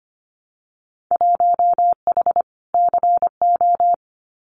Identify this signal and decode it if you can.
Morse code, heard as a keyed tone, about 25 wpm, 700 Hz. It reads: 1H CO
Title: Winter Field Day Exchanges 25wpm